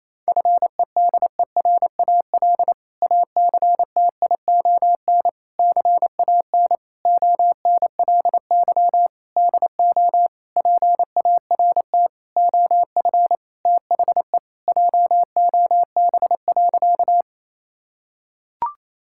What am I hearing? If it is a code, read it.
Morse code, heard as a keyed tone, about 28 wpm, 700 Hz. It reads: FEDERAL ACTION CAN ONLY DO PART OF THE JOB.
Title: Sentences from Top 1000 Words - Review Part 5 - 28wpm